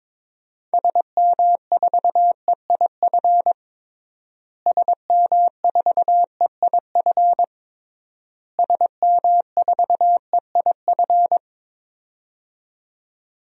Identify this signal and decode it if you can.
Morse code, heard as a keyed tone, about 22 wpm, 700 Hz. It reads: SM4EIF SM4EIF SM4EIF
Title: DXCC-100 Call Signs - 6 Characters - Repeated 3x - 22wpm